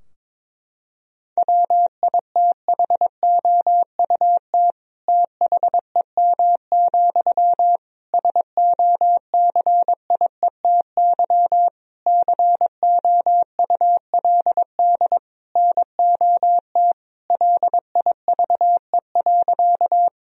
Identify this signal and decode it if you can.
Morse code, heard as a keyed tone, about 22 wpm, 700 Hz. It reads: WITHOUT THEM, SOCIETY COULD NOT LIVE.
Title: Sentences from Top 1000 Words - Review Part 4 - 22wpm